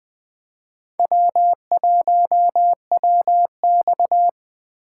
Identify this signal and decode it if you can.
Morse code, heard as a keyed tone, about 20 wpm, 700 Hz. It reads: W1WX
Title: USA Call Signs - 4 Characters - 2x Character Spacing - 20wpm